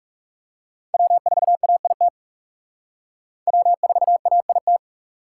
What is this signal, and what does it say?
Morse code, heard as a keyed tone, about 37 wpm, 700 Hz. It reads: W4AIT W4AIT
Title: Speed Racing - USA 1x3 Call Signs 30wpm